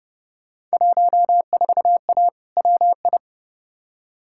Morse code, heard as a keyed tone, about 29 wpm, 700 Hz. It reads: 14A WI
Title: Field Day Exchanges Encore - 30wpm